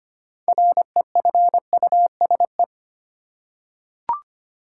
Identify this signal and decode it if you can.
Morse code, heard as a keyed tone, about 25 wpm, 700 Hz. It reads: REFUSE
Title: Six Letter Words - 25wpm